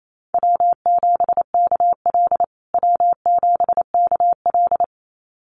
Morse code, heard as a keyed tone, about 28 wpm, 700 Hz. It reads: W7KL W7KL
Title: USA Call Signs Repeated 2x 28wpm